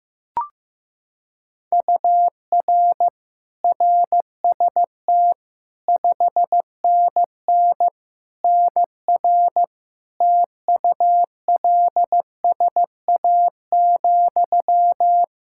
Morse code, heard as a keyed tone, about 15 wpm, 700 Hz. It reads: UR RST 5NN NR TULSA,
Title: Practice QSO - Exchange 1 - Beginning - 15wpm